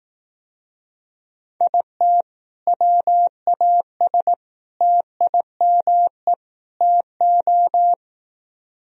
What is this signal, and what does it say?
Morse code, heard as a keyed tone, about 18 wpm, 700 Hz. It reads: IT WAS TIME TO
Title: Speed-Racing - Sets of 4 Words 15wpm